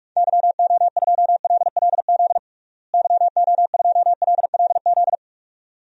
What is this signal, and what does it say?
Morse code, heard as a keyed tone, about 46 wpm, 700 Hz. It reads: YO2LLZ YO2LLZ
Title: DXCC-100 Call Signs - 6 Characters - Repeated 2x - 45wpm